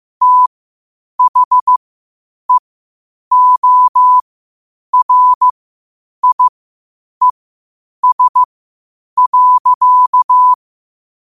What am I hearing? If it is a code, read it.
Morse code, heard as a keyed tone, about 15 wpm, spaced as for 8 wpm, 1000 Hz. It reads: THEORIES.
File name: qotdfiles / 8wpm